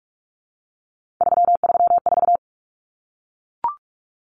Morse code, heard as a keyed tone, about 44 wpm, 700 Hz. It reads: 334
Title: Numbers - 3-Digit 45wpm